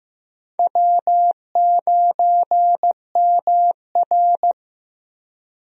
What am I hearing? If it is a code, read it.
Morse code, heard as a keyed tone, about 15 wpm, 700 Hz. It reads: W9MR
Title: USA Call Signs - 4 Characters - Repeated 3x - 15wpm